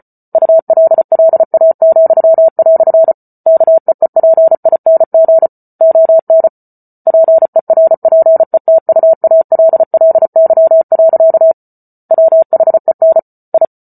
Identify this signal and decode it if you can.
Morse code, heard as a keyed tone, about 34 wpm, 650 Hz. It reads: ULLA," KEEPING ON PERPETUALLY. WHEN I